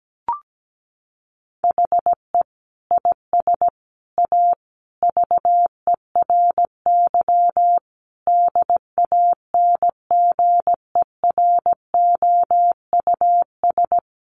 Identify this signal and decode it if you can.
Morse code, heard as a keyed tone, about 17 wpm, 700 Hz. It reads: HE IS A VERY DANGEROUS
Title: Sentences from Top 1200 Words 17wpm